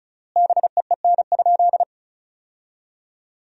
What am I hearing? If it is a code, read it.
Morse code, heard as a keyed tone, about 34 wpm, 700 Hz. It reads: BEEN?
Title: 3x Word-Spacing - Sentences from Top 400 Words - 35wpm